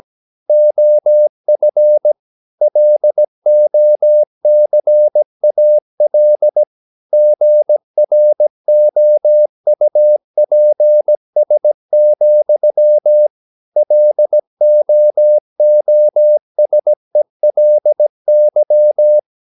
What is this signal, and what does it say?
Morse code, heard as a keyed tone, about 17 wpm, 600 Hz. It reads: OF LOCAL GROUPS, LOOSELY
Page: Romanen Never af Ken Follet på engelsk.